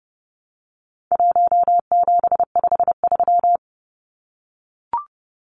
Morse code, heard as a keyed tone, about 30 wpm, 700 Hz. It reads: 1753
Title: Numbers - 4-Digit 30wpm